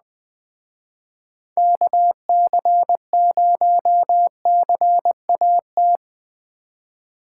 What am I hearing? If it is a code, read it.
Morse code, heard as a keyed tone, about 20 wpm, 700 Hz. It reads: KC0CAT
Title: Speed Racing - USA 2x3 Call Signs 17wpm